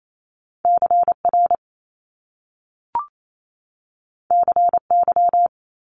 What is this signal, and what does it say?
Morse code, heard as a keyed tone, about 28 wpm, 700 Hz. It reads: CR CY
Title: Top Two Letter Combinations in Words 28wpm